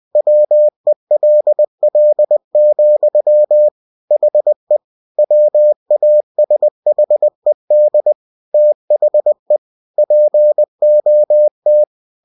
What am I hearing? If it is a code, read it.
Morse code, heard as a keyed tone, about 20 wpm, 600 Hz. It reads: WELL, HE WASHED THE POT
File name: cw-books / winnie-the-pooh / winnie-20wpm-page-0051